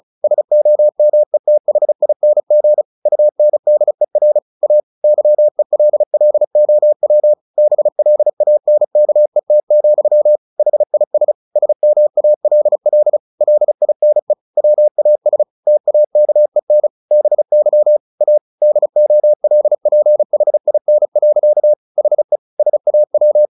Morse code, heard as a keyed tone, about 34 wpm, 600 Hz. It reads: SOMETHING UNDER A YELLOW BLANKET, HIS SMALL LINE WAS TAKEN BY A DOLPHIN. HE SAW
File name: the old man and the sea 2 35wpm